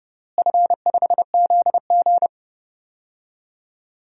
Morse code, heard as a keyed tone, about 30 wpm, 700 Hz. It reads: F5ZG
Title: DXCC-100 Call Signs - 4 Characters - Repeated 2x - 30wpm